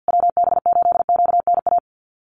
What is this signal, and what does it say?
Morse code, heard as a keyed tone, about 50 wpm, 700 Hz. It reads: WB8QNA
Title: USA Call Signs Repeated 3x 50wpm